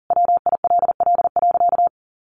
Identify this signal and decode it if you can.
Morse code, heard as a keyed tone, about 40 wpm, 700 Hz. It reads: WILL.
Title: Sentences from Top 1500 Words 40wpm